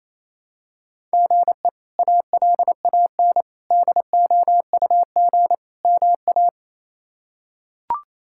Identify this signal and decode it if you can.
Morse code, heard as a keyed tone, about 28 wpm, 700 Hz. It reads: GE ALAN DOUG MA
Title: K1USN SST QSO Practice - 28wpm